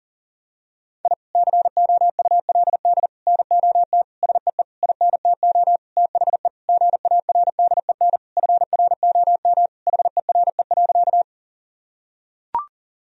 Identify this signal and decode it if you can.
Morse code, heard as a keyed tone, about 40 wpm, 700 Hz. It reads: I COULD NOT SEE INTO THE GARDEN FROM HERE.
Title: Sentences from Top 1000 Words - Review Part 4 - 40wpm